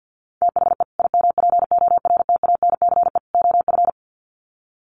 Morse code, heard as a keyed tone, about 50 wpm, 700 Hz. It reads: THE IMPORTANCE OF
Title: Sets of 3 Words 50wpm